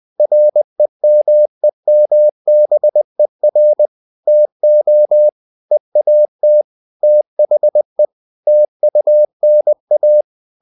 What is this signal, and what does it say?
Morse code, heard as a keyed tone, about 20 wpm, 600 Hz. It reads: REMEMBER TO EAT THE TUNA
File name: the old man and the sea / the old man and the sea part 1 20wpm